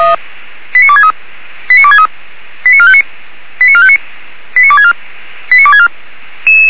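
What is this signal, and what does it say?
Five-tone selective call (ZVEI), heard as quick runs of five tones, tones of about 70 ms. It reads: 89263 89263 89459 89459 89264 89264
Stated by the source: ZVEI five-tone selective call received by SDR.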